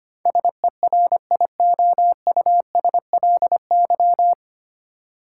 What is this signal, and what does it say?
Morse code, heard as a keyed tone, about 25 wpm, 700 Hz. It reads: SERIOUSLY
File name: practice / 25wpm / 9 Letter Words - First 3 Letters Given - 25wpm